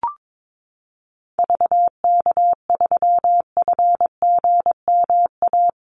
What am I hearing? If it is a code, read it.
Morse code, heard as a keyed tone, about 22 wpm, 700 Hz. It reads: VK3FGMA
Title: USA Call Signs Repeated 2x 22wpm